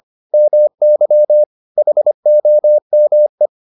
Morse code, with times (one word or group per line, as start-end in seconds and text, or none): 0.3-1.4 MY
1.8-3.5 HOME